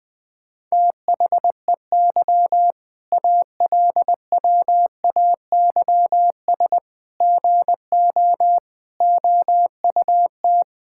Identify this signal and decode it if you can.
Morse code, heard as a keyed tone, about 20 wpm, 700 Hz. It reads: THEY ALWAYS GO OUT
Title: Sentences with 6-Word Phrases - 20wpm